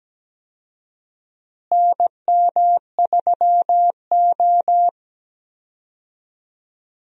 Morse code, heard as a keyed tone, about 17 wpm, 700 Hz. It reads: NM3O
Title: USA Call Signs - 4 Characters - Repeated 3x - 17wpm